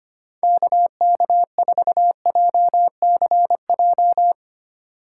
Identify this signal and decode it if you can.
Morse code, heard as a keyed tone, about 31 wpm, 700 Hz. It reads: KK4JCJ
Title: Speed Racing - USA 2x3 Call Signs 25wpm